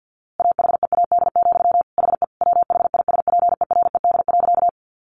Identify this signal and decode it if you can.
Morse code, heard as a keyed tone, about 50 wpm, 700 Hz. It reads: AHEAD, HE WHISPERED.
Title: A Sound Of Thunder 50wpm